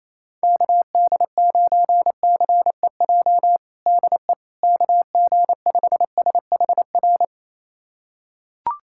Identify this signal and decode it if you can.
Morse code, heard as a keyed tone, about 28 wpm, 700 Hz. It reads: KD9CEJ DE KG5SHR
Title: USA Callsign DE Callsign - 6 Characters - 28wpm